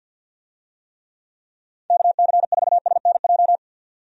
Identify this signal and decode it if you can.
Morse code, heard as a keyed tone, about 47 wpm, 700 Hz. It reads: KC4SNJ
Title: USA Call Signs - 6 Characters - 50wpm